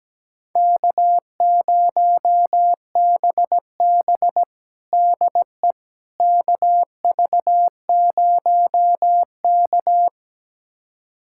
Morse code, heard as a keyed tone, about 17 wpm, 700 Hz. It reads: K0BB DE KV0K
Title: USA Callsign DE Callsign - 4 Characters - 17wpm